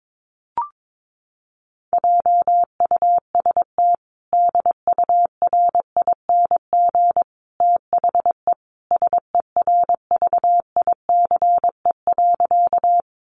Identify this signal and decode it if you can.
Morse code, heard as a keyed tone, about 22 wpm, 700 Hz. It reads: JUST DURING THE SERVICE.
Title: Sentences from Top 700 Words 22wpm